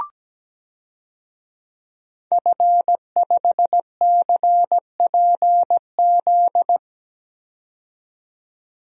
Morse code, heard as a keyed tone, about 17 wpm, 700 Hz. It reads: F5CPZ
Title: DXCC-100 Call Signs - 5 Characters - 17wpm